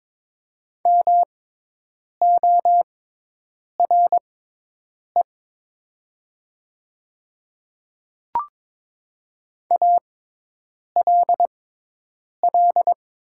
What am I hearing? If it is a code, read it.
Morse code, heard as a keyed tone, about 22 wpm, spaced as for 8 wpm, 700 Hz. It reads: MORE ALL
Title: 6x Character Spacing - Top 200 Words - 22wpm